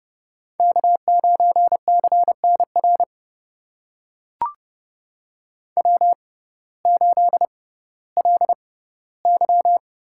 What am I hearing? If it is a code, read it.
Morse code, heard as a keyed tone, about 29 wpm, spaced as for 16 wpm, 700 Hz. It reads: K9CNR W8LY
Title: USA Call Signs - 5 Characters - 6x Character Spacing - 30wpm